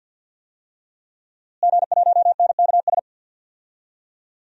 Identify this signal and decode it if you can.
Morse code, heard as a keyed tone, about 48 wpm, 700 Hz. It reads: G1NKS
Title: DXCC-100 Call Signs - 5 Characters - Repeated 3x - 50wpm